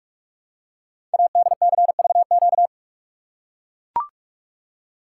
Morse code, heard as a keyed tone, about 44 wpm, 700 Hz. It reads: ADCVQ
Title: Code Groups - 5 Letters - 45wpm